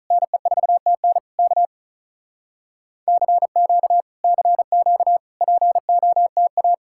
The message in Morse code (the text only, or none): NE4TN K CQ CQ POTA